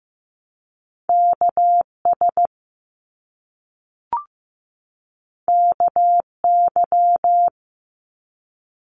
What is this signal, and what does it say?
Morse code, heard as a keyed tone, about 15 wpm, 700 Hz. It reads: KS KY
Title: US States 15wpm